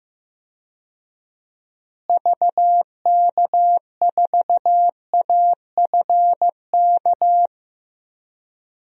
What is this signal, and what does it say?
Morse code, heard as a keyed tone, about 15 wpm, 700 Hz. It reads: VK4AFK
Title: DXCC-100 Call Signs - 6 Characters - Repeated 3x - 15wpm